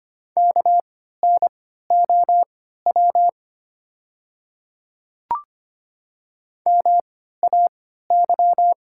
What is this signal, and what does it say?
Morse code, heard as a keyed tone, about 25 wpm, spaced as for 14 wpm, 700 Hz. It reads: KNOW MAY
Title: 3x Character Spacing - Top 200 Words - 25wpm